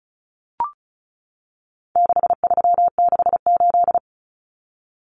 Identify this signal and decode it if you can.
Morse code, heard as a keyed tone, about 34 wpm, 700 Hz. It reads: 6368
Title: Common Cut Numbers - 4-Digit 35wpm